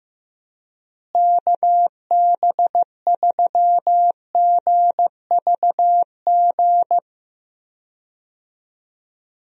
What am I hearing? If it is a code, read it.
Morse code, heard as a keyed tone, about 15 wpm, 700 Hz. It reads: KB3GVG
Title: USA Call Signs - 6 Characters - Repeated 3x - 15wpm